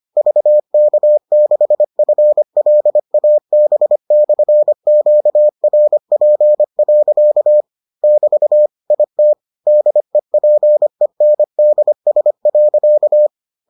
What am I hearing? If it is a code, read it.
Morse code, heard as a keyed tone, about 25 wpm, 600 Hz. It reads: VK6FLAB/QRP. <BT> IT DEPENDS.